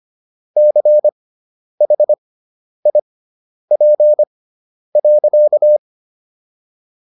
Morse code, heard as a keyed tone, about 25 wpm, spaced as for 10 wpm, 600 Hz. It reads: CHIP.